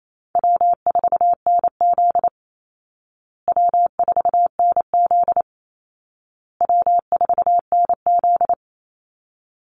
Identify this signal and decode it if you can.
Morse code, heard as a keyed tone, about 28 wpm, 700 Hz. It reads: W4NZ W4NZ W4NZ
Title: Call Signs Repeated 2x - 28wpm